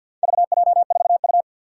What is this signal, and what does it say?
Morse code, heard as a keyed tone, about 47 wpm, 700 Hz. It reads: VJ4U
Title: Four Challenging Letter-Number Combinations 50wpm